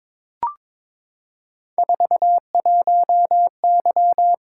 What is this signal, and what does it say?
Morse code, heard as a keyed tone, about 22 wpm, 700 Hz. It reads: 41Y
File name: practice / 22wpm / Three Letter-Number Combinations - 22wpm